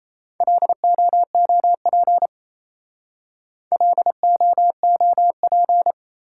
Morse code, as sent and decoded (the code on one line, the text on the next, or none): .-.. --- --- .--. / .-.. --- --- .--.
LOOP LOOP